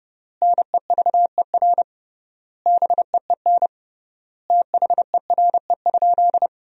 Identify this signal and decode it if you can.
Morse code, heard as a keyed tone, about 30 wpm, 700 Hz. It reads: NEVER BEEN THERE?
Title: 3x Word-Spacing - Sentences from Top 300 Words - 30wpm